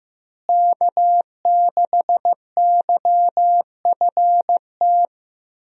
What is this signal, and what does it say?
Morse code, heard as a keyed tone, about 14 wpm, 700 Hz. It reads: K6YFT
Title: USA Call Signs - 5 Characters - Repeated 3x - 15wpm